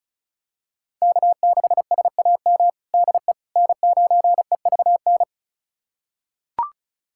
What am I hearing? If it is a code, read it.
Morse code, heard as a keyed tone, about 36 wpm, 700 Hz. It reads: K6SAM DE N9EVN
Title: USA Callsign DE Callsign - 5 Characters - 35wpm